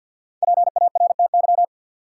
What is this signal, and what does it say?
Morse code, heard as a keyed tone, about 53 wpm, 700 Hz. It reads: PARTY
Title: Sentences from Top 1400 Words - New Word Drill - 50wpm